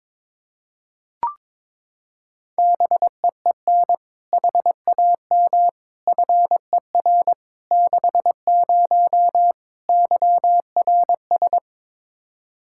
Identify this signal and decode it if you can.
Morse code, heard as a keyed tone, about 21 wpm, 700 Hz. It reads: BEEN HAM FER 60 YRS
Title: K1USN SST QSO Practice 22wpm